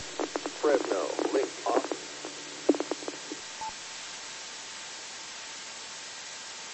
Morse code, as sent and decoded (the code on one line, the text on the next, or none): none